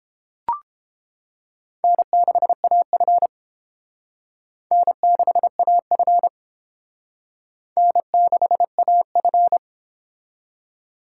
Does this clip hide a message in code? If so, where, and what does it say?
Yes: Morse code from 1.8 to 9.6 s, heard as a keyed tone, about 29 wpm, 700 Hz. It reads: N6AF N6AF N6AF